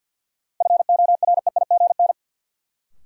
Morse code, heard as a keyed tone, about 52 wpm, 700 Hz. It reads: FOREIGN
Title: Sentences from Top 1400 Words - New Word Drill - 50wpm